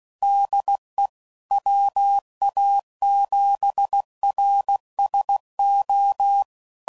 Morse code, heard as a keyed tone, about 16 wpm, 800 Hz. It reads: DE WA7RSO
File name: CW Code Groups @15WPM / QSOs / kc7haa